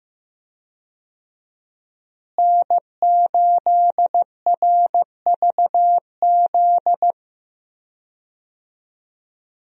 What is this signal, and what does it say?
Morse code, heard as a keyed tone, about 15 wpm, 700 Hz. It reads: N8RVZ
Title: USA Call Signs - 5 Characters - Repeated 2x - 15wpm